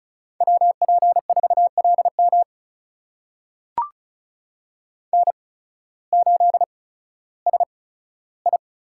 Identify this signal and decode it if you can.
Morse code, heard as a keyed tone, about 36 wpm, spaced as for 16 wpm, 700 Hz. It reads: WP4LM N8SI
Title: USA Call Signs - 5 Characters - 8x Character Spacing - 35wpm